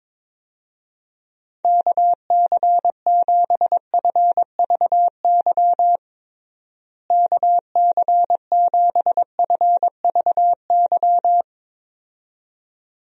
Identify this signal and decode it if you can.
Morse code, heard as a keyed tone, about 22 wpm, 700 Hz. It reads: KC7FVY KC7FVY
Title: USA Call Signs - 6 Characters - Repeated 2x - 22wpm